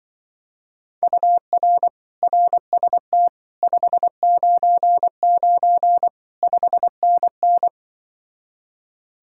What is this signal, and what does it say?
Morse code, heard as a keyed tone, about 25 wpm, 700 Hz. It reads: UR RST 599 5NN
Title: Speed-Racing Common QSO Phrases - 20wpm